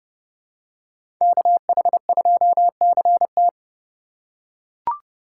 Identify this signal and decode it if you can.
Morse code, heard as a keyed tone, about 30 wpm, 700 Hz. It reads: KH2CT
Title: USA Call Signs - 5 Characters - 3x Character Spacing - 30wpm